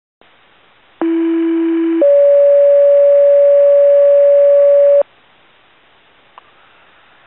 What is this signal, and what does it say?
Two-tone page, heard as a one-second tone, then a three-second tone, about 330 then 570 Hz. It is heard from 1.0 to 5.0 s.